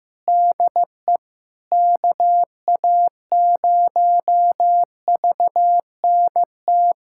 Morse code, heard as a keyed tone, about 15 wpm, 700 Hz. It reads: DE KA0VNT
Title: USA Callsign DE Callsign - 6 Characters - 15wpm